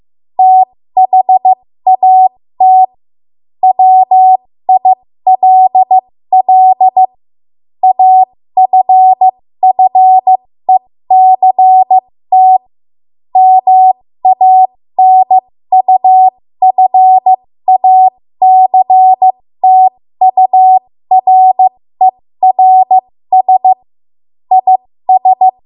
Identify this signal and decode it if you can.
Morse code, heard as a keyed tone, about 15 wpm, spaced as for 13 wpm, 750 Hz. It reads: THAT WILL AFFECT MANUFACTURERS IS
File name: Morse / 13WPM